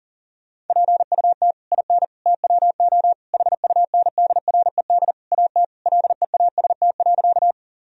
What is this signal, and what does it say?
Morse code, heard as a keyed tone, about 40 wpm, 700 Hz. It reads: PUT IN TWO HUNDRED AT LEAST.
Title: Sentences from Top 1000 Words - Review Part 2 - 40wpm